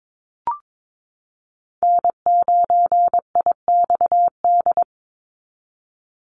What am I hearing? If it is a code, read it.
Morse code, heard as a keyed tone, about 22 wpm, 700 Hz. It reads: N9IXD
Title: Call Signs Repeated 2x - 22wpm